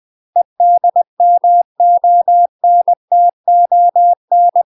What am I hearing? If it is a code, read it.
Morse code, heard as a keyed tone, about 20 wpm, 700 Hz. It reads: EDMONTON
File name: UK MASTER QSOs TRAINING / 20WPM / UK MASTER QSOs 0015